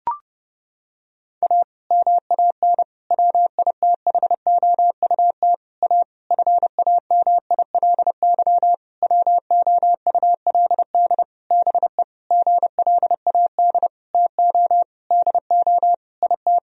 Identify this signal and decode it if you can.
Morse code, heard as a keyed tone, about 32 wpm, 700 Hz. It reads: A MAN WITHOUT A FAMILY WOULD BE GLAD TO DO IT
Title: Sentences from Top 1000 Words - Review Part 3 - 30wpm